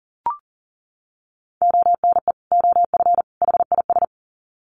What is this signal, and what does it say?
Morse code, heard as a keyed tone, about 40 wpm, 700 Hz. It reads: ONE OF HIS
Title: Sets of 3 Words 40wpm